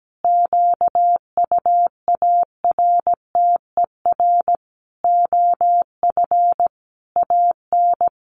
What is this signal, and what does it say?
Morse code, heard as a keyed tone, about 17 wpm, 700 Hz. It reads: QUARTER OF AN
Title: Sentences from Top 1300 Words 17wpm